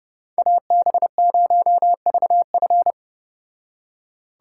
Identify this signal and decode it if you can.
Morse code, heard as a keyed tone, about 33 wpm, 700 Hz. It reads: AB0VF